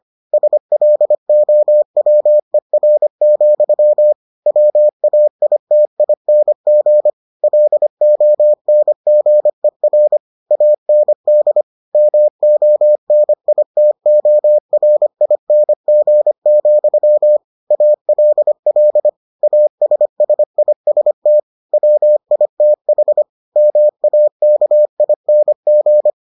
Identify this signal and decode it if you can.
Morse code, heard as a keyed tone, about 25 wpm, 600 Hz. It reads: SLOWER, WAITING LONGER AND MONITORING, ALL ASSIST WITH MAKING